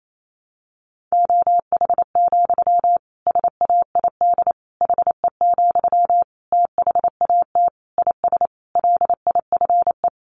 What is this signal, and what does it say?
Morse code, heard as a keyed tone, about 28 wpm, 700 Hz. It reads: OH, SAID HE, THAT IS LIFE
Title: Sentences from Top 500 Words 28wpm